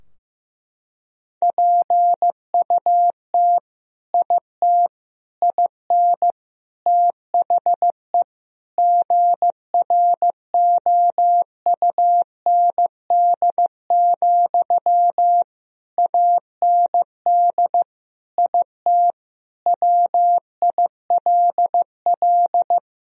Morse code, heard as a keyed tone, about 15 wpm, 700 Hz. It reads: PUT IT IN THE GROUND, AND IT WILL
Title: Sentences from Top 1000 Words - Review Part 2 - 15wpm